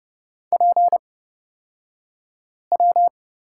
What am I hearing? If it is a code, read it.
Morse code, heard as a keyed tone, about 30 wpm, spaced as for 5 wpm, 700 Hz. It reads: PW
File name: Lesson 38d - Letter q - Two Characters 30wpm